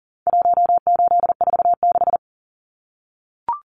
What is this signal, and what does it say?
Morse code, heard as a keyed tone, about 40 wpm, 700 Hz. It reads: 1846
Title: Numbers - Contest Style - 4-Digit 40wpm